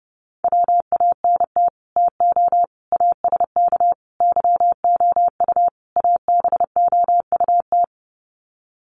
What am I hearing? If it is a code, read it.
Morse code, heard as a keyed tone, about 30 wpm, 700 Hz. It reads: WANT TO ASK YOU ABOUT
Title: Sets of 5 Words Encore2 30wpm